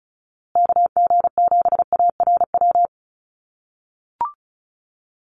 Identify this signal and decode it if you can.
Morse code, heard as a keyed tone, about 34 wpm, 700 Hz. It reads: KG7ARW
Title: USA Call Signs Repeated 3x 35wpm